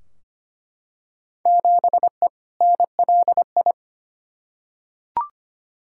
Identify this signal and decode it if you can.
Morse code, heard as a keyed tone, about 25 wpm, 700 Hz. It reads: 7E NLI
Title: Field Day Exchanges Encore - 25wpm